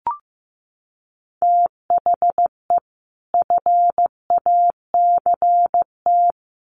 Morse code, heard as a keyed tone, about 15 wpm, 700 Hz. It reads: THE FACT
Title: Sets of 2 Words 15wpm